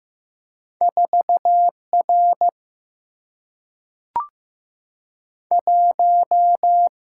Morse code, heard as a keyed tone, about 15 wpm, 700 Hz. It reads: 4 R 1
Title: Single-Letter Single-Number 15wpm